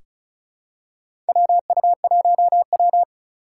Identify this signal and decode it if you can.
Morse code, heard as a keyed tone, about 34 wpm, 700 Hz. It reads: WU1W
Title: USA Call Signs - 4 Characters - 8x Character Spacing - 35wpm